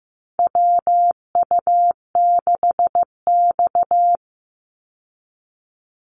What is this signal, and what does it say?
Morse code, heard as a keyed tone, about 15 wpm, 700 Hz. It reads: WU6X
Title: Call Signs Repeated 2x - 15wpm